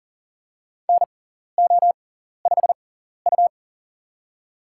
Morse code, heard as a keyed tone, about 37 wpm, spaced as for 14 wpm, 700 Hz. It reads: NO5U
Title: USA Call Signs - 4 Characters - 6x Character Spacing - 40wpm